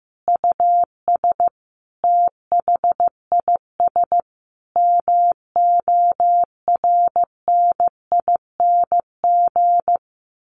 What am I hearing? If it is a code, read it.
Morse code, heard as a keyed tone, about 15 wpm, 700 Hz. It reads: US THIS MORNING
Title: Sets of 5 Words Encore 15wpm